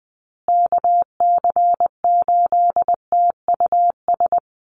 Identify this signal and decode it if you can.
Morse code, heard as a keyed tone, about 19 wpm, 700 Hz. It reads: KC8TUS
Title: USA Call Signs Repeated 2x 20wpm